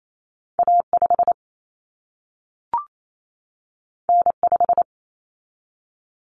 Morse code, heard as a keyed tone, about 28 wpm, 700 Hz. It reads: A5 N5
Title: Common Cut Numbers - 2-Digit 28wpm